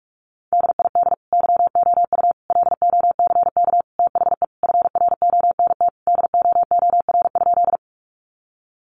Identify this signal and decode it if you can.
Morse code, heard as a keyed tone, about 44 wpm, 700 Hz. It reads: DID YOU LOCK THE FRONT DOOR?
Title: Sentences from Top 1300 Words 45wpm